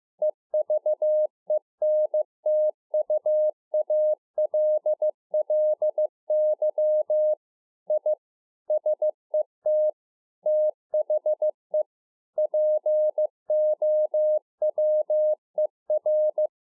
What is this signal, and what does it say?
Morse code, heard as a keyed tone, about 15 wpm, 610 Hz. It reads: EVENTUALLY I SET THE POWER
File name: 2011-04-mp3 / A SMALL TRANCEIVER 2 15WPM